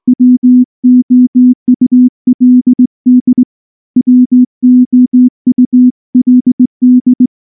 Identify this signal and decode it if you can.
Morse code, heard as a keyed tone, about 20 wpm, 250 Hz. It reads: WOULD WOULD
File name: morse / morse-training-20WPM